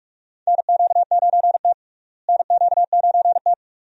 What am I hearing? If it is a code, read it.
Morse code, heard as a keyed tone, about 42 wpm, 700 Hz. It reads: NQ9T NQ9T
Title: USA Call Signs - 4 Characters - Repeated 2x - 45wpm